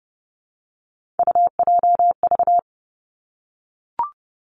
Morse code, heard as a keyed tone, about 31 wpm, 700 Hz. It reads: UJV
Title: Three Challenging Letter-Number Combinations 30wpm